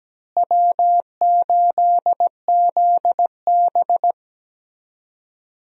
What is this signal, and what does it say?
Morse code, heard as a keyed tone, about 17 wpm, 700 Hz. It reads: W8ZB
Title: USA Call Signs - 4 Characters - Repeated 3x - 17wpm